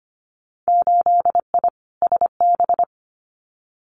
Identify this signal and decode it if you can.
Morse code, heard as a keyed tone, about 25 wpm, 700 Hz. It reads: 8I SB
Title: Winter Field Day Exchanges Encore 25wpm